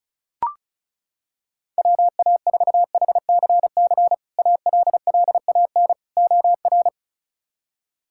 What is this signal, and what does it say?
Morse code, heard as a keyed tone, about 36 wpm, 700 Hz. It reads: WA4HCC ALLAN OR
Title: K1USN SST QSO Practice - 35wpm